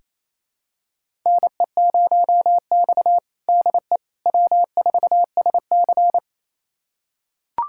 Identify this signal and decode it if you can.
Morse code, heard as a keyed tone, about 28 wpm, 700 Hz. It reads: NE0X DE W4SC
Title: USA Callsign DE Callsign - 4 Characters - 28wpm